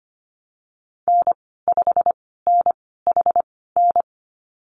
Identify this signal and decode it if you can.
Morse code, heard as a keyed tone, about 25 wpm, spaced as for 16 wpm, 700 Hz. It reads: N5NHN
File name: Lesson 15g - Number 5 - Call Signs 25wpm